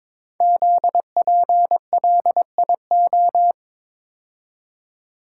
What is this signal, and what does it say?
Morse code, heard as a keyed tone, about 22 wpm, 700 Hz. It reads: ZPLIO
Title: Code Groups - 5 Letters - 22wpm